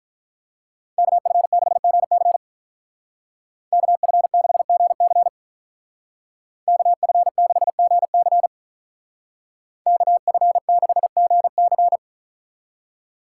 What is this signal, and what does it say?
Morse code, heard as a keyed tone, about 43 wpm, 700 Hz. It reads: KF6GC KF6GC KF6GC KF6GC
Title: Speed Racing - USA 2x2 Call Signs 35wpm